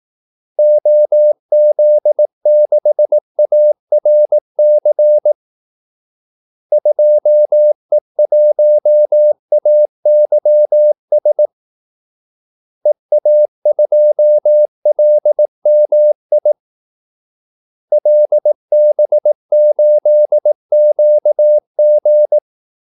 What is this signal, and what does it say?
Morse code, heard as a keyed tone, about 18 wpm, 600 Hz. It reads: OZ6ARC 2E1AYS EA2LMI LB8QG
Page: Kaldesignaler 18 wpm